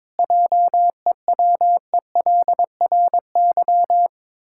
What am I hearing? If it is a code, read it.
Morse code, heard as a keyed tone, about 22 wpm, 700 Hz. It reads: JEWELRY
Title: Pangram Challenge v2 - 22wpm